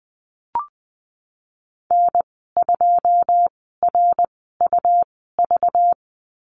Morse code, heard as a keyed tone, about 20 wpm, spaced as for 14 wpm, 700 Hz. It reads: N2RUV
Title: 193 Lesson 31g - Letter v - Call Signs 20wpm